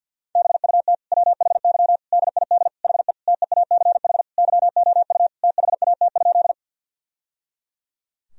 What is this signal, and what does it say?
Morse code, heard as a keyed tone, about 52 wpm, 700 Hz. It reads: BUT WHY DID HE TEACH YOU THAT?
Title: Sentences from Top 1000 Words - Review Part 2 - 50wpm